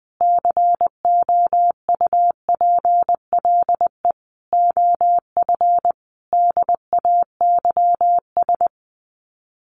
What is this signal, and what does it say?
Morse code, heard as a keyed tone, about 20 wpm, 700 Hz. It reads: COUPLE OF DAYS
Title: Sets of 5 Words Encore2 20wpm